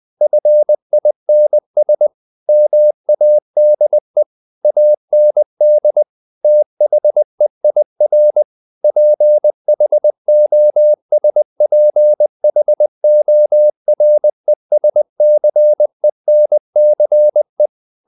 Morse code, heard as a keyed tone, about 20 wpm, 600 Hz. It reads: FINS MADE AND THEIR PHOSPHORESCENCE
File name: the old man and the sea part 2 20wpm